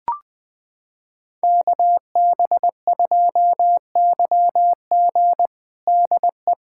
Morse code, heard as a keyed tone, about 20 wpm, 700 Hz. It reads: KB2YG DE
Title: USA Callsign DE Callsign - 5 Characters - 20wpm